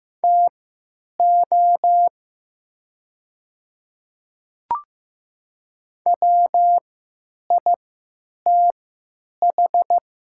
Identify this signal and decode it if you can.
Morse code, heard as a keyed tone, about 15 wpm, spaced as for 9 wpm, 700 Hz. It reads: TO WITH
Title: 3x Character Spacing - Top 100 Words - 15wpm